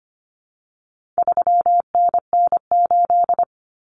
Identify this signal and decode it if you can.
Morse code, heard as a keyed tone, about 25 wpm, 700 Hz. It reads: 3NN8
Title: Common Cut Numbers - 4-Digit 25wpm